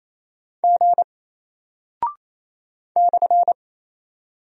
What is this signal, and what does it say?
Morse code, heard as a keyed tone, about 27 wpm, 700 Hz. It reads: G /
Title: Single Character (Letter:Number:Punctuation:Prosign) 28wpm